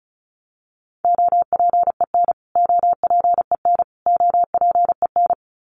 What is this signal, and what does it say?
Morse code, heard as a keyed tone, about 34 wpm, 700 Hz. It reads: OPEN OPEN OPEN
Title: Top 500 Words Repeated 35wpm